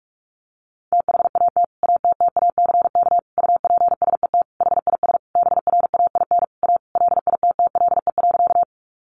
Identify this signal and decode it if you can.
Morse code, heard as a keyed tone, about 44 wpm, 700 Hz. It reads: THAT ATTACK UPSET HIS BRAIN A LITTLE.
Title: Sentences from Top 1500 Words 45wpm